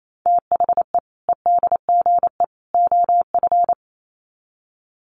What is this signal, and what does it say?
Morse code, heard as a keyed tone, about 28 wpm, 700 Hz. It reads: THE EDGE OF
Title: Sets of 3 Words 28wpm